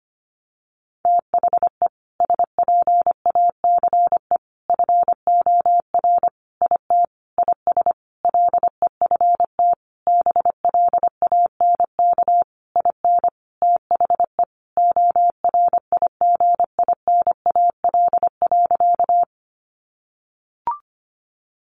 Morse code, heard as a keyed tone, about 25 wpm, 700 Hz. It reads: THE SPACE FOR IT IS LEFT BLANK IN THE ORIGINAL.
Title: Sentences from Top 1500 Words 25wpm